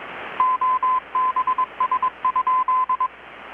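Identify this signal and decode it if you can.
Morse code, heard as a keyed tone, about 21 wpm, 1020 Hz. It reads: OBS?
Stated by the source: Morse from KPH's Night of Nights XX maritime radiotelegraphy event, received in Chiapas, Mexico about 3,800 km away on a Kenwood R-600 with a grounded 10 m long-wire antenna.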